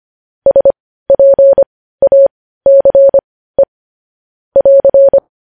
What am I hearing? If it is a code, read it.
Morse code, heard as a keyed tone, about 25 wpm, spaced as for 15 wpm, 550 Hz. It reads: SPACE <AR>
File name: podcast / News headlines-1x-25x15